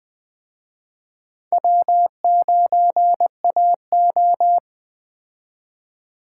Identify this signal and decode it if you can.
Morse code, heard as a keyed tone, about 20 wpm, 700 Hz. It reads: W9AO
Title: Speed Racing - USA 1x2 Call Signs 17wpm